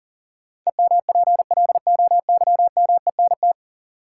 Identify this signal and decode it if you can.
Morse code, heard as a keyed tone, about 40 wpm, 700 Hz. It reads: EMPLOYMENT
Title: Sentences from Top 1500 Words - New Word Drill - 40wpm